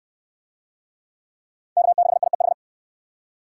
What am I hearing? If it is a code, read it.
Morse code, heard as a keyed tone, about 60 wpm, 700 Hz. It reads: K6IEH